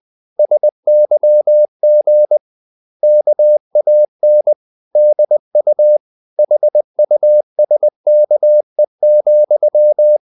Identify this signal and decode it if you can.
Morse code, heard as a keyed tone, about 20 wpm, 600 Hz. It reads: SYG KAN DU HUSKE,
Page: Ferie 20 wpm